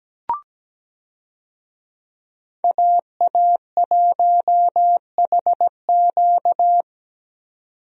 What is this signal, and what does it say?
Morse code, heard as a keyed tone, about 17 wpm, 700 Hz. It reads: AA1HQ
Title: USA Call Signs - 5 Characters - 17wpm